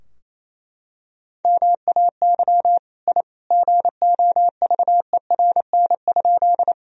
Morse code, heard as a keyed tone, about 27 wpm, 700 Hz. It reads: MAY I GOVERN?
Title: Sentences from Top 1000 Words - Review Part 2 - 28wpm